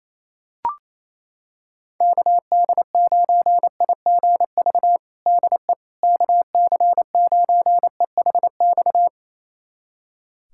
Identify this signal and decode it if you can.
Morse code, heard as a keyed tone, about 28 wpm, 700 Hz. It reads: KD9IGV DE KC9EHX
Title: USA Callsign DE Callsign - 6 Characters - 28wpm